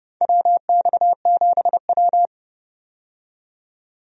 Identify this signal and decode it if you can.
Morse code, heard as a keyed tone, about 30 wpm, 700 Hz. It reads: WX7W
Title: USA Call Signs - 4 Characters - 30wpm